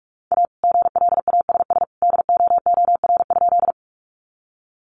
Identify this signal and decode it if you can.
Morse code, heard as a keyed tone, about 44 wpm, 700 Hz. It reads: A GLASS DOOR?
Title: Sentences from Top 1300 Words 45wpm